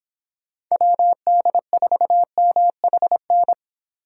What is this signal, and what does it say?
Morse code, heard as a keyed tone, about 25 wpm, 700 Hz. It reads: WD4MHN